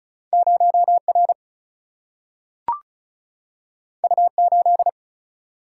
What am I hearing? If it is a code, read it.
Morse code, heard as a keyed tone, about 34 wpm, 700 Hz. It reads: 0R U8
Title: Single-Letter Single-Number 35wpm